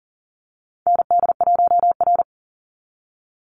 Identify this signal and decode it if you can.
Morse code, heard as a keyed tone, about 40 wpm, 700 Hz. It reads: ND1R
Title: Call Signs - 40wpm